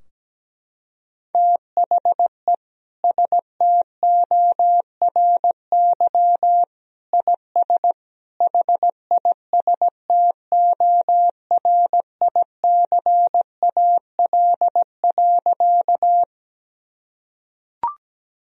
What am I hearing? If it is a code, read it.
Morse code, heard as a keyed tone, about 17 wpm, 700 Hz. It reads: THE STORY IS HISTORICAL.
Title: Sentences from Top 1000 Words - Review Part 4 - 17wpm